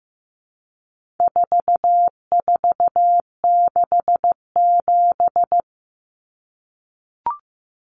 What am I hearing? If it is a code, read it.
Morse code, heard as a keyed tone, about 15 wpm, 700 Hz. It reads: 4467
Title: Numbers - 4-Digit 15wpm